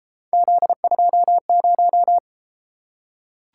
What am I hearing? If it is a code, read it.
Morse code, heard as a keyed tone, about 32 wpm, 700 Hz. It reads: Z20